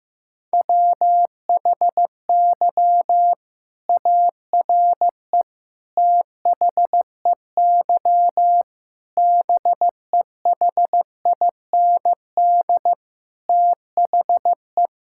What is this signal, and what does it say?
Morse code, heard as a keyed tone, about 15 wpm, 700 Hz. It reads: WHY ARE THEY BEHIND THE
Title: Sentences from Top 1000 Words - Review Part 6 - 15wpm